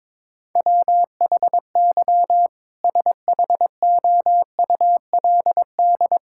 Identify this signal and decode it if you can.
Morse code, heard as a keyed tone, about 22 wpm, 700 Hz. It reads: WHY SHOULD
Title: Sentences from Top 1000 Words - Review Part 6 - 22wpm